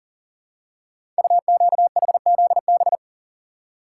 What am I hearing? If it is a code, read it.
Morse code, heard as a keyed tone, about 42 wpm, 700 Hz. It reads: UQHZB
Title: Code Groups - 5 Letters - 40wpm